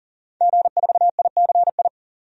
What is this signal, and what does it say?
Morse code, heard as a keyed tone, about 40 wpm, 700 Hz. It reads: G4ICI
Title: DXCC-100 Call Signs - 5 Characters - 40wpm